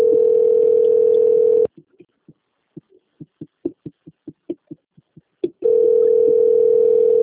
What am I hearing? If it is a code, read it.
Ringback tone (North America), 440 + 480 Hz.